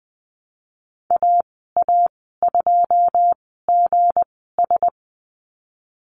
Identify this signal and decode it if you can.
Morse code, heard as a keyed tone, about 19 wpm, spaced as for 14 wpm, 700 Hz. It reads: AA2GS
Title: 167 Lesson 27g - Number 7 - Call Signs 20wpm